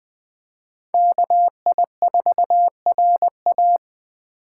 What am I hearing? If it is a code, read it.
Morse code, heard as a keyed tone, about 20 wpm, 700 Hz. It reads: KI4RA
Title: Speed Racing - USA 2x2 Call Signs 17wpm